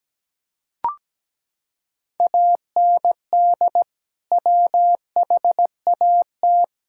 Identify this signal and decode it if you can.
Morse code, heard as a keyed tone, about 17 wpm, 700 Hz. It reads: AND WHAT
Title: Sentences from Top 1000 Words - Review Part 5 - 17wpm